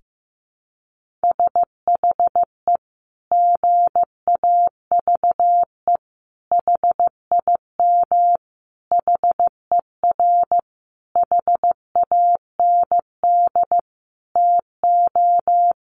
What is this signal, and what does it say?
Morse code, heard as a keyed tone, about 15 wpm, 700 Hz. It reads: SHE GAVE HIM HER HAND TO
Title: Sentences from Top 1400 Words 15wpm